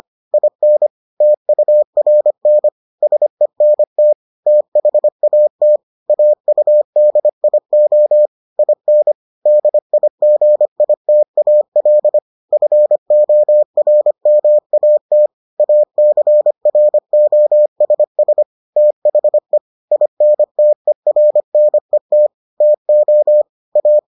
Morse code, heard as a keyed tone, about 25 wpm, 600 Hz. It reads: IN TURN SENT THAT AUDIO IN DIGITAL FORMAT ACROSS THE INTERNET TO A